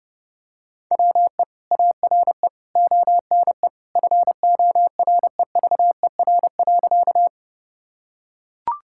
Morse code, heard as a keyed tone, about 30 wpm, 700 Hz. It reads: WE ARE ONE FOREVER.
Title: Sentences from Top 1600 Words 30wpm